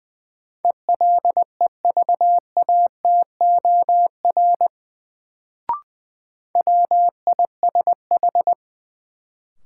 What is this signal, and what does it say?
Morse code, heard as a keyed tone, about 20 wpm, 700 Hz. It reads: ELEVATOR WISH
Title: Sentences from Top 1600 Words - New Word Drill - 20wpm